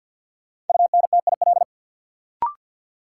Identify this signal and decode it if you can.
Morse code, heard as a keyed tone, about 53 wpm, 700 Hz. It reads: UNTIL
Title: Sentences from Top 500 Words - New Word Drill - 50wpm